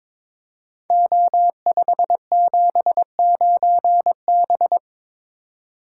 Morse code, heard as a keyed tone, about 23 wpm, 700 Hz. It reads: O579B
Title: Code Groups - 5 Letter:Numbers - 22wpm